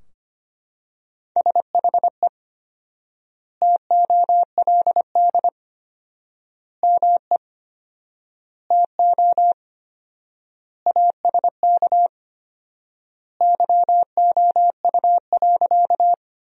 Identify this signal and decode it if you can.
Morse code, heard as a keyed tone, about 25 wpm, 700 Hz. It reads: SHE TOLD ME TO ASK YOU.
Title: 4x Word-Spacing - Sentences from Top 400 Words - 25wpm